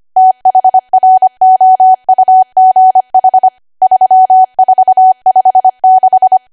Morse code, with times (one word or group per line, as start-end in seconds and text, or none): 0.2-3.5 THROUGH
3.8-6.4 3456